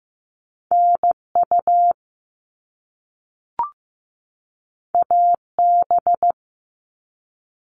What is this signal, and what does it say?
Morse code, heard as a keyed tone, about 15 wpm, 700 Hz. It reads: NU AB
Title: Top Two Letter Combinations in Words 15wpm